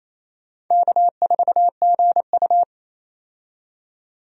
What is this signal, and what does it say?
Morse code, heard as a keyed tone, about 28 wpm, 700 Hz. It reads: K4GU
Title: USA Call Signs - 4 Characters - 28wpm